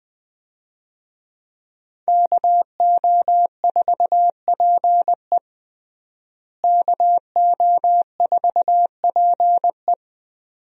Morse code, heard as a keyed tone, about 21 wpm, 700 Hz. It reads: KO4PE KO4PE
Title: USA Call Signs - 5 Characters - Repeated 2x - 20wpm